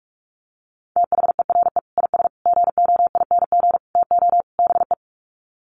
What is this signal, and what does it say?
Morse code, heard as a keyed tone, about 46 wpm, 700 Hz. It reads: THERE IS GOING TO BE
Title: Sets of 5 Words Encore 45wpm